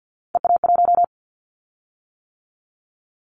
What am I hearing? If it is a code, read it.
Morse code, heard as a keyed tone, about 50 wpm, 700 Hz. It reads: EA1
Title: Callsign Prefixes for Top 25 DXCC - 50wpm